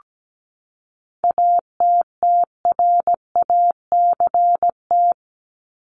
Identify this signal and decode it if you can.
Morse code, heard as a keyed tone, about 17 wpm, 700 Hz. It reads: ATTRACT
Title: Top 1501-1750 Words 17wpm